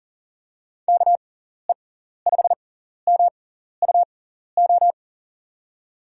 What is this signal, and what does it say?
Morse code, heard as a keyed tone, about 40 wpm, spaced as for 14 wpm, 700 Hz. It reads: KE5MUO
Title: USA Call Signs - 6 Characters - 6x Character Spacing - 40wpm